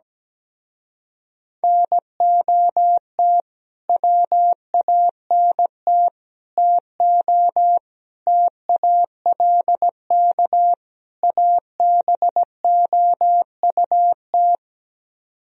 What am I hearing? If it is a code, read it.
Morse code, heard as a keyed tone, about 17 wpm, 700 Hz. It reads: NOT WANT TO TALK ABOUT
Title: Speed-Racing - Sets of 5 Words 17wpm